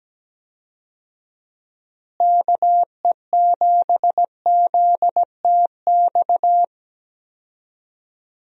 Morse code, heard as a keyed tone, about 17 wpm, 700 Hz. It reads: KE7ZTX
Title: USA Call Signs - 6 Characters - Repeated 3x - 17wpm